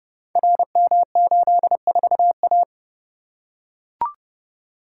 Morse code, heard as a keyed tone, about 30 wpm, 700 Hz. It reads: RM84A
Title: Code Groups - 5 Letter:Numbers - 30wpm